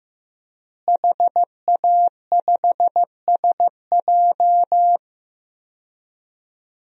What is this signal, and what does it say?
Morse code, heard as a keyed tone, about 15 wpm, 700 Hz. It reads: HA5SJ
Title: DXCC-100 Call Signs - 5 Characters - Repeated 3x - 15wpm